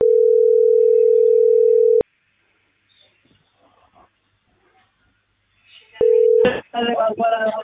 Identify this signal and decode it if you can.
Ringback tone (North America), 440 + 480 Hz.